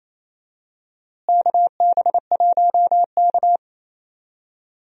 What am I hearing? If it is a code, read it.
Morse code, heard as a keyed tone, about 28 wpm, 700 Hz. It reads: KB1K
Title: USA Call Signs - 4 Characters - Repeated 3x - 28wpm